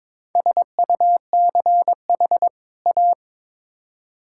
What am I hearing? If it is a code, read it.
Morse code, heard as a keyed tone, about 22 wpm, 700 Hz. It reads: SUCH A
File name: Build A Phrase - 2-Word Phrases - 22wpm